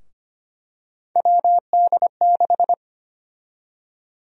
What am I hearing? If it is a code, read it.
Morse code, heard as a keyed tone, about 25 wpm, 700 Hz. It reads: WD6
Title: USA Call Sign Prefixes - 25wpm